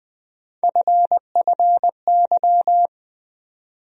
Morse code, heard as a keyed tone, about 20 wpm, 700 Hz. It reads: FFY
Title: USA Call Sign Postfixes - 3 Letters 20wpm